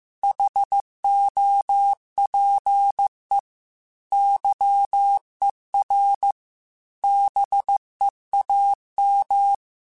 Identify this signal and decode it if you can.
Morse code, heard as a keyed tone, about 15 wpm, 800 Hz. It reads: HOPE YER BEAM
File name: QSOs / QSOs-mp3s / WD0BGP 2@13